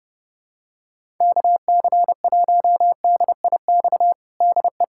Morse code, heard as a keyed tone, about 30 wpm, 700 Hz. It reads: KC1DIX DE
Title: USA Callsign DE Callsign 30wpm